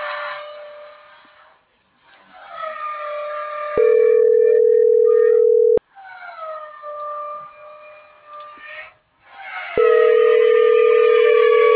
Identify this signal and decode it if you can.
Ringback tone (North America), 440 + 480 Hz.